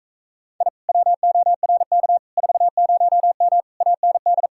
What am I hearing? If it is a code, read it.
Morse code, heard as a keyed tone, about 36 wpm, 700 Hz. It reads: I WORK 40M AND